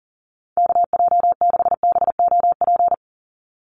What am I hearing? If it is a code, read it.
Morse code, heard as a keyed tone, about 40 wpm, 700 Hz. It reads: KJ6BOP
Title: USA Call Signs 40wpm